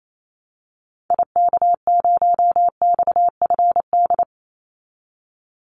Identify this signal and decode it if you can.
Morse code, heard as a keyed tone, about 28 wpm, 700 Hz. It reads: IK0XFD
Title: Call Signs Repeated 3x - 28wpm